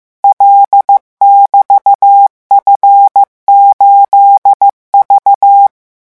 Morse code, heard as a keyed tone, about 15 wpm, 800 Hz. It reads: L<BT>F8V
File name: morse mp3 / revision13-vitesse 14 mots